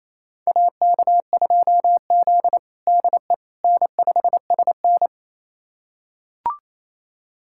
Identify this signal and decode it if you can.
Morse code, heard as a keyed tone, about 28 wpm, 700 Hz. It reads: AK2Z DE N5SN
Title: USA Callsign DE Callsign - 4 Characters - 28wpm